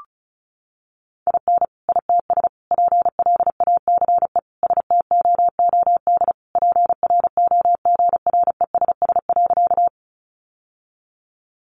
Morse code, heard as a keyed tone, about 34 wpm, 700 Hz. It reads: IN ITS PLACE STOOD PROGRESS.
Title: Sentences from Top 1400 Words 35wpm